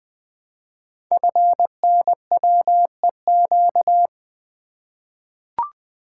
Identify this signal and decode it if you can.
Morse code, heard as a keyed tone, about 20 wpm, 700 Hz. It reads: FNWEQ
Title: Code Groups - 5 Letters - 20wpm